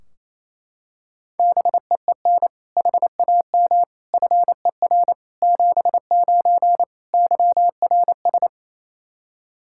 Morse code, heard as a keyed tone, about 28 wpm, 700 Hz. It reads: BEEN HAM FER 79 YRS
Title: Practice QSO - Exchange 3 - More Info - 28wpm